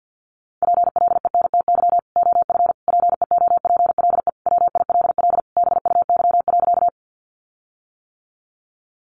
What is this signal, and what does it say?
Morse code, heard as a keyed tone, about 50 wpm, 700 Hz. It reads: PLENTY OF PEOPLE WILL BUY.
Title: Sentences from Top 1300 Words 50wpm